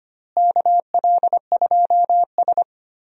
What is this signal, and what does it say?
Morse code, heard as a keyed tone, about 25 wpm, 700 Hz. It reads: KL2S
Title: USA Call Signs - 4 Characters - Repeated 2x - 25wpm